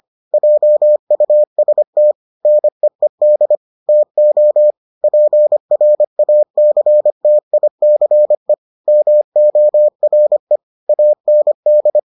Morse code, heard as a keyed tone, about 25 wpm, 600 Hz. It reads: JUST NEED TO PRACTICE MORE AND